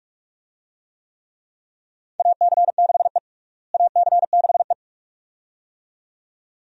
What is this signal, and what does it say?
Morse code, heard as a keyed tone, about 46 wpm, 700 Hz. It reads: AC6E AC6E
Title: USA Call Signs - 4 Characters - Repeated 2x - 45wpm